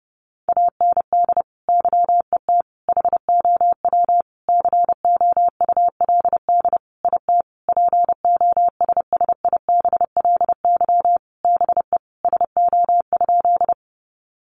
Morse code, heard as a keyed tone, about 30 wpm, 700 Hz. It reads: AND YET HOW COULD IT POSSIBLY BE SO?
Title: Sentences from Top 1300 Words 30wpm